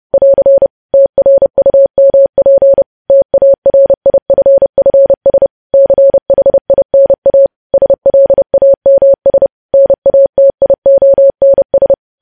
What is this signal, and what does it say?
Morse code, heard as a keyed tone, about 30 wpm, 550 Hz. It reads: <AR> TRUMP TARIFFS CHINA SLAMS NATIONS
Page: News Headlines in Morse Code at 30 WPM